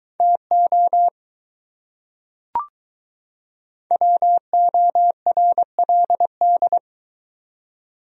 Morse code, heard as a keyed tone, about 22 wpm, 700 Hz. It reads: TO WORLD